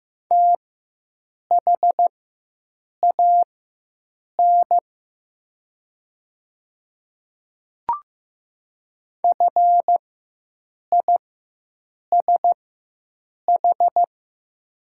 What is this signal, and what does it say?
Morse code, heard as a keyed tone, about 15 wpm, spaced as for 7 wpm, 700 Hz. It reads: THAN FISH
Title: 4x Character Spacing - Top 300 Words - 15wpm